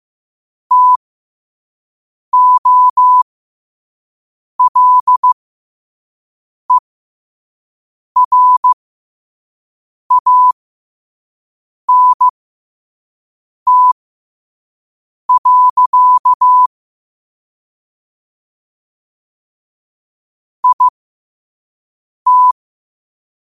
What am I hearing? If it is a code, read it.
Morse code, heard as a keyed tone, about 15 wpm, spaced as for 5 wpm, 1000 Hz. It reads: TOLERANT. IT
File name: qotdfiles / 5wpm